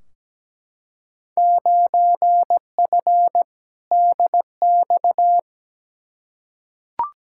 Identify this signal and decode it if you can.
Morse code, heard as a keyed tone, about 17 wpm, 700 Hz. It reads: 9F DX
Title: Field Day Exchanges Encore - 17wpm